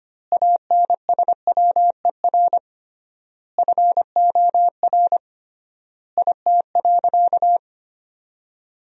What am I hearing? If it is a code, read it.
Morse code, heard as a keyed tone, about 25 wpm, 700 Hz. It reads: ANSWER FOR IT.
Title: 3x Word-Spacing - Sentences from Top 300 Words - 25wpm